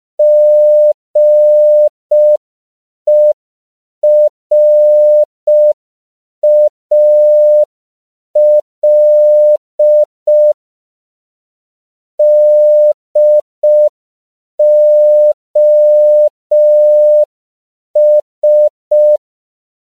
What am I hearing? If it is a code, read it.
Morse code, heard as a keyed tone, about 5 wpm, 600 Hz. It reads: GERAL DOS
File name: CW / 5PPM